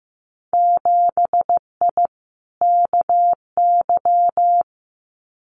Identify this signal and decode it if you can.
Morse code, heard as a keyed tone, about 15 wpm, 700 Hz. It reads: 7I KY
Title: Winter Field Day Exchanges Encore 15wpm